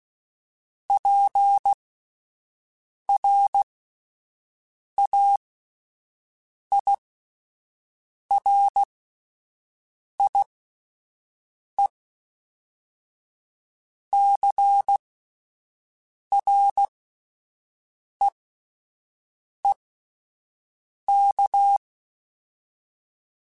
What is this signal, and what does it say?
Morse code, heard as a keyed tone, about 16 wpm, spaced as for 5 wpm, 800 Hz. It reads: PRAIRIECREEK